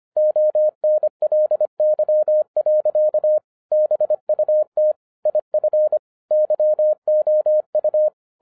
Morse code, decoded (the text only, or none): ONLY. BUT IF YOU